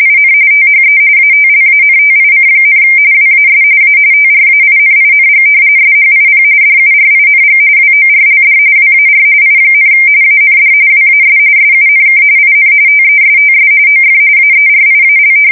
RTTY, about 45.45 baud. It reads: IN THE ATLANTIC, WAS A TRAINING SHIP DURING WORLD WAR I, AND WAS SUNK IN TESTS OF AERIAL
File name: fsk170 / battleshipnj